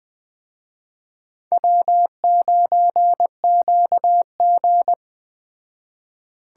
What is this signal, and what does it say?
Morse code, heard as a keyed tone, about 20 wpm, 700 Hz. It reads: W9QG